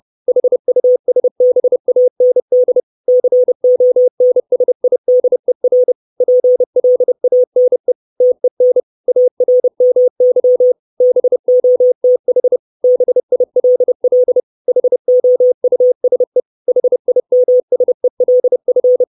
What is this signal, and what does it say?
Morse code, heard as a keyed tone, about 30 wpm, 500 Hz. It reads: HUSBAND CONSIDER PLANE TEN ARMY BOTH BILL HOUSE HIMSELF
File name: 632 Words 30WPM